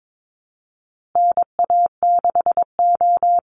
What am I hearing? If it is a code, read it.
Morse code, heard as a keyed tone, about 22 wpm, 700 Hz. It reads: NA6O
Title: Call Signs Repeated 2x - 22wpm